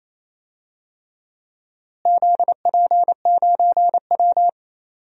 Morse code, heard as a keyed tone, about 28 wpm, 700 Hz. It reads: ZP9W
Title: DXCC-100 Call Signs - 4 Characters - Repeated 3x - 28wpm